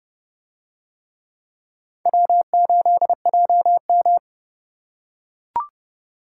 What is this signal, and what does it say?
Morse code, heard as a keyed tone, about 29 wpm, 700 Hz. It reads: W8JM